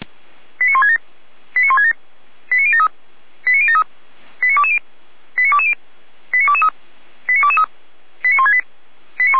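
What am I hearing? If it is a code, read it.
Five-tone selective call (ZVEI), heard as quick runs of five tones, tones of about 70 ms. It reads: 89167 89167 89073 89073 89220 89220 89223 89223 89168
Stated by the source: ZVEI five-tone call 89022 received by SDR, logged by Feuerwehr Orlamünde.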